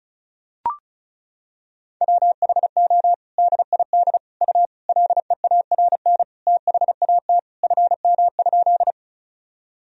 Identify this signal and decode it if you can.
Morse code, heard as a keyed tone, about 34 wpm, 700 Hz. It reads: WHO DID U LEARN THAT FM?
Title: Sentences from Top 300 Words and Abbreviations 35wpm